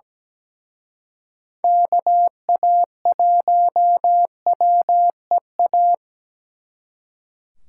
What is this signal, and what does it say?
Morse code, heard as a keyed tone, about 20 wpm, 700 Hz. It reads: KA1WEA
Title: Speed Racing - USA 2x3 Call Signs 17wpm